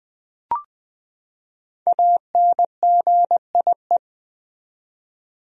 Morse code, heard as a keyed tone, about 21 wpm, 700 Hz. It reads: ANGIE
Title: USA Common Names - Max 5 Letters - 20wpm